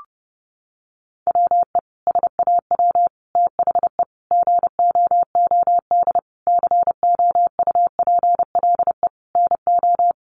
Morse code, heard as a keyed tone, about 30 wpm, 700 Hz. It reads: WE SAW THE GOOD COUPLE NO
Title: Sentences from Top 1100 Words 30wpm